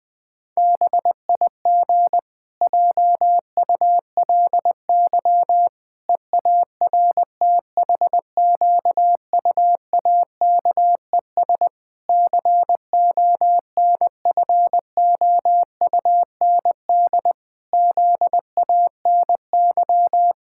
Morse code, heard as a keyed tone, about 20 wpm, 700 Hz. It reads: BIG JULY EARTHQUAKES CONFOUND ZANY
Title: Pangram Challenge v2 - 20wpm